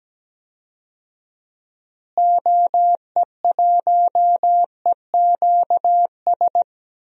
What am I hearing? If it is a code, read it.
Morse code, heard as a keyed tone, about 17 wpm, 700 Hz. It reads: OE1EQS
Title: DXCC-100 Call Signs - 6 Characters - Repeated 3x - 17wpm